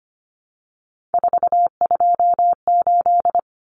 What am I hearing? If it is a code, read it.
Morse code, heard as a keyed tone, about 25 wpm, 700 Hz. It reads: 428
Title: Common Cut Numbers - 3-Digit 25wpm